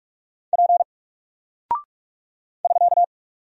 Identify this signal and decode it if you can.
Morse code, heard as a keyed tone, about 44 wpm, 700 Hz. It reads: P <SK>
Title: Single Character (Letter:Number:Punctuation:Prosign) 45wpm